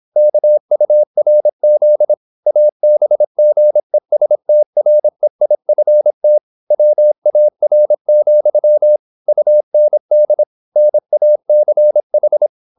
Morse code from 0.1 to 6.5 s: KURZ ABGESTREIFT